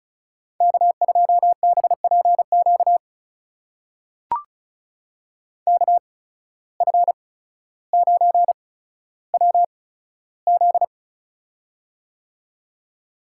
Morse code, heard as a keyed tone, about 36 wpm, spaced as for 15 wpm, 700 Hz. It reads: K2BPQ KF9WZ
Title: USA Call Signs - 5 Characters - 8x Character Spacing - 35wpm